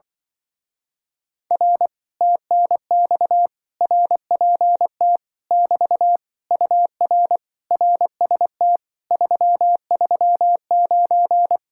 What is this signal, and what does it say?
Morse code, heard as a keyed tone, about 25 wpm, 700 Hz. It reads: R TNX RPT <BT> UR RST 339
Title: Speed-Racing Common QSO Phrases - 20wpm